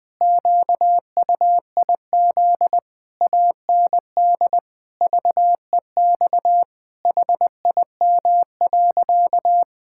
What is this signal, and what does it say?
Morse code, heard as a keyed tone, about 20 wpm, 700 Hz. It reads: QUIZ AND VEX HIM.
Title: Pangram Challenge v2 - 20wpm